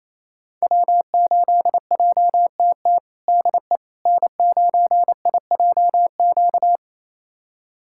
Morse code, heard as a keyed tone, about 28 wpm, 700 Hz. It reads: W8JTT DE N9IJQ
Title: USA Callsign DE Callsign - 5 Characters - 28wpm